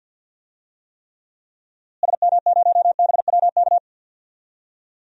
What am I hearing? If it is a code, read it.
Morse code, heard as a keyed tone, about 50 wpm, 700 Hz. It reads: SM0BWK
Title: DXCC-100 Call Signs - 6 Characters - Repeated 3x - 50wpm